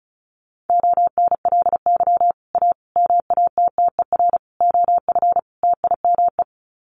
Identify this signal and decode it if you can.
Morse code, heard as a keyed tone, about 34 wpm, 700 Hz. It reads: ONLY A MATTER OF TIME
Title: Sets of 5 Words Encore 35wpm